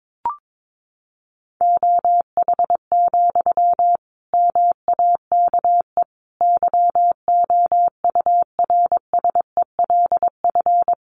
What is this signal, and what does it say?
Morse code, heard as a keyed tone, about 22 wpm, 700 Hz. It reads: OH, MAKE YOURSELF
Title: Sentences from Top 700 Words 22wpm